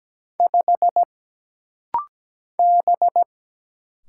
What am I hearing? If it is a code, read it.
Morse code, heard as a keyed tone, about 17 wpm, 700 Hz. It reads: 5 B
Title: Single Letter-Number - Rapid-Fire 17wpm